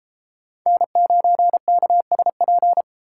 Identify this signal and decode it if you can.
Morse code, heard as a keyed tone, about 29 wpm, 700 Hz. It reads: N9KSP